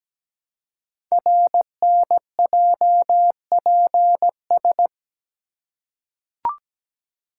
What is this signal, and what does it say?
Morse code, heard as a keyed tone, about 17 wpm, 700 Hz. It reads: RNJPS
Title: Code Groups - 5 Letters - 17wpm